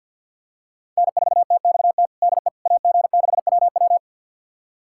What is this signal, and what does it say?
Morse code, heard as a keyed tone, about 52 wpm, 700 Hz. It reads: N4TXT DE AG6WW
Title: USA Callsign DE Callsign - 5 Characters - 50wpm